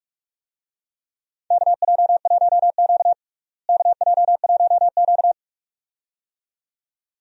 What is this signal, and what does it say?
Morse code, heard as a keyed tone, about 46 wpm, 700 Hz. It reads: KJ1Q KJ1Q
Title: USA Call Signs - 4 Characters - Repeated 2x - 45wpm